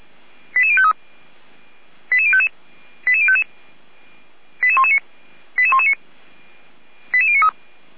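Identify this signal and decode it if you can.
Five-tone selective call (ZVEI), heard as quick runs of five tones, tones of about 70 ms. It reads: 88053 88055 88055 88119 88119 88093
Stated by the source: ZVEI five-tone selective call received by SDR, logged by the Feuerwehr Orlamünde fire brigade.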